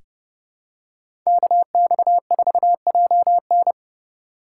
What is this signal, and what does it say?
Morse code, heard as a keyed tone, about 30 wpm, 700 Hz. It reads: KX4JN
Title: USA Call Signs - 5 Characters - 3x Character Spacing - 30wpm